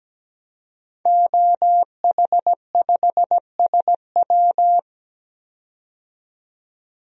Morse code, heard as a keyed tone, about 17 wpm, 700 Hz. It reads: OH5SW
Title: DXCC-100 Call Signs - 5 Characters - 17wpm